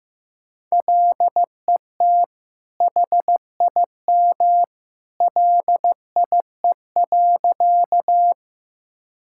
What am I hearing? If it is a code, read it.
Morse code, heard as a keyed tone, about 15 wpm, 700 Hz. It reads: LET HIM LIE.
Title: Sentences from Top 1600 Words 15wpm